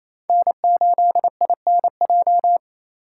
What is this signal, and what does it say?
Morse code, heard as a keyed tone, about 36 wpm, 700 Hz. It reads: N8INJ